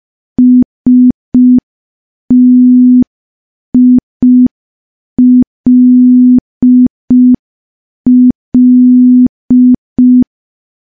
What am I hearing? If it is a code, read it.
Morse code, heard as a keyed tone, about 5 wpm, 250 Hz. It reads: STILL
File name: morse / morse-training-05WPM